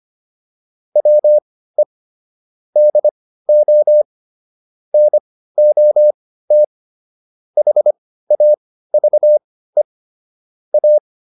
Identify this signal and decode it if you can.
Morse code, heard as a keyed tone, about 25 wpm, spaced as for 15 wpm, 600 Hz. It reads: WE DO NOT HAVE A